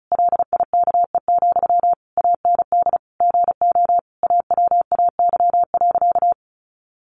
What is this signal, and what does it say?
Morse code, heard as a keyed tone, about 34 wpm, 700 Hz. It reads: LIKE, AND GO AWAY.
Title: Sentences from Top 1100 Words 35wpm